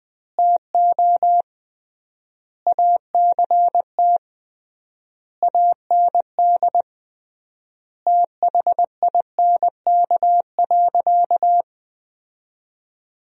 Morse code, heard as a keyed tone, about 21 wpm, 700 Hz. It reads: TO ACT AND THINK.
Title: 3x Word-Spacing - Sentences from Top 500 Words - 20wpm